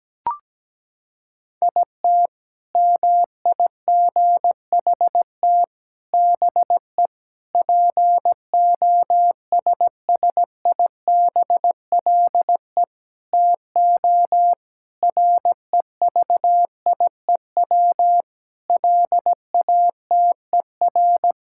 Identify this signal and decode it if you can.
Morse code, heard as a keyed tone, about 17 wpm, 700 Hz. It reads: IT MIGHT BE POSSIBLE TO REVIEW LATER
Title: Sentences from Top 1000 Words - Review Part 6 - 17wpm